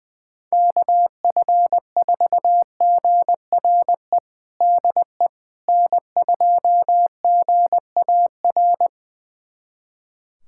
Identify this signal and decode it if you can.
Morse code, heard as a keyed tone, about 20 wpm, 700 Hz. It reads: KF4GRE DE N2GAR
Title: USA Callsign DE Callsign 20wpm